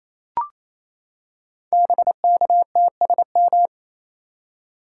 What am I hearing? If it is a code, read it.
Morse code, heard as a keyed tone, about 29 wpm, 700 Hz. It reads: BKTSM
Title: Code Groups - 5 Letters - 28wpm